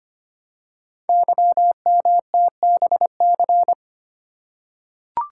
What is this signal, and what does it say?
Morse code, heard as a keyed tone, about 25 wpm, 700 Hz. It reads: YMTBC
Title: Code Groups - 5 Letters - 25wpm